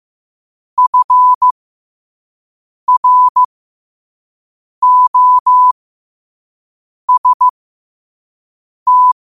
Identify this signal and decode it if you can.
Morse code, heard as a keyed tone, about 15 wpm, spaced as for 5 wpm, 1000 Hz. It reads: FROST
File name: qotdfiles / 5wpm